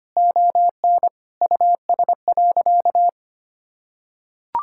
Morse code, heard as a keyed tone, about 25 wpm, 700 Hz. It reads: ON US.
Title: Sentences from Top 1000 Words - Review Part 6 - 25wpm